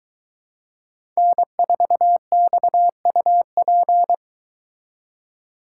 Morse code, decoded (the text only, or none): N4XUP